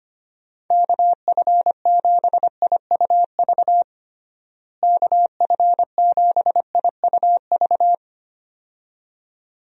Morse code, heard as a keyed tone, about 25 wpm, 700 Hz. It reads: KF7IUV KF7IUV
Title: USA Call Signs - 6 Characters - Repeated 2x - 25wpm